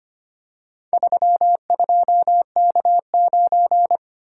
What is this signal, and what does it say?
Morse code, heard as a keyed tone, about 25 wpm, 700 Hz. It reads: 32K9
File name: practice / 25wpm / Four Letter-Number Combinations - 25wpm